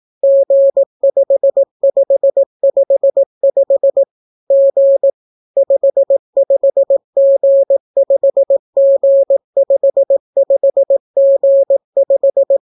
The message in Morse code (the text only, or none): G5555 G 55G5G55G5